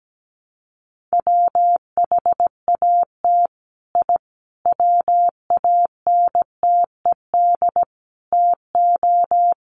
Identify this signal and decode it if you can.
Morse code, heard as a keyed tone, about 17 wpm, 700 Hz. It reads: WHAT I WANTED TO
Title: Sets of 5 Words Encore2 17wpm